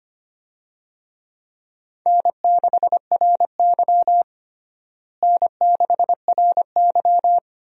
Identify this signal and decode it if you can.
Morse code, heard as a keyed tone, about 25 wpm, 700 Hz. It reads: N6RY N6RY
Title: USA Call Signs - 4 Characters - Repeated 3x - 25wpm